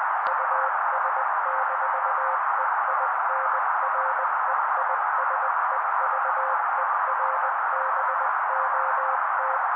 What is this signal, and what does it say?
Morse code, heard as a keyed tone, about 20 wpm, 530 Hz. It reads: US-EINREISEVERBOT
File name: daily / 18wpm noise